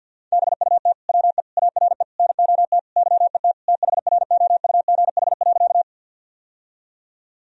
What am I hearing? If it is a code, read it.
Morse code, heard as a keyed tone, about 51 wpm, 700 Hz. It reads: BUT WE ARE NOT YET THROUGH.
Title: Sentences from Top 1000 Words - Review Part 1 - 50wpm